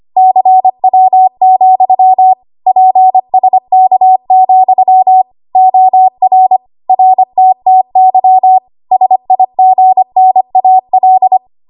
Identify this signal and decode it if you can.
Morse code, heard as a keyed tone, about 25 wpm, 750 Hz. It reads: CW, PSK, OR RTTY SIGNAL
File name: Archive / 25 WPM / 25WPM